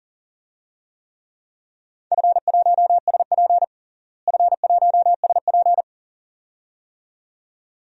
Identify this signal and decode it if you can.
Morse code, heard as a keyed tone, about 37 wpm, 700 Hz. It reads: F1SP F1SP
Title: DXCC-100 Call Signs - 4 Characters - Repeated 2x - 40wpm